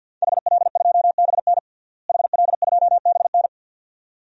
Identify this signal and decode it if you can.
Morse code, heard as a keyed tone, about 49 wpm, 700 Hz. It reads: HL2BN HL2BN
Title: DXCC-100 Call Signs - 5 Characters - Repeated 2x - 50wpm